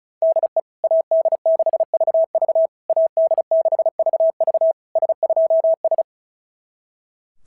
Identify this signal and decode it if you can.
Morse code, heard as a keyed tone, about 33 wpm, 650 Hz. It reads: DE AD6VV AD6VV S2S
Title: SOTA QSO Practice 35wpm